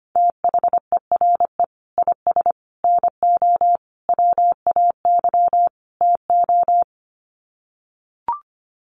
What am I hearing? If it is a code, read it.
Morse code, heard as a keyed tone, about 25 wpm, 700 Hz. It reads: THERE IS NO WAY TO
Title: Sets of 5 Words Encore 25wpm